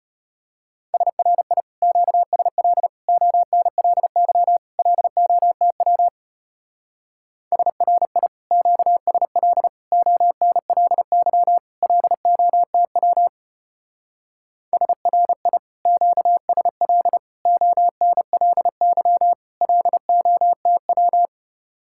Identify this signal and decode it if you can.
Morse code, heard as a keyed tone, about 30 wpm, 700 Hz. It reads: SRI QSL ONLY LOTW SRI QSL ONLY LOTW SRI QSL ONLY LOTW
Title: Speed-Racing Common QSO Phrases - 25wpm